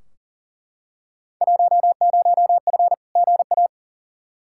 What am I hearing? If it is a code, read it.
Morse code, heard as a keyed tone, about 40 wpm, 700 Hz. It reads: 10F GA
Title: Field Day Exchanges Encore - 40wpm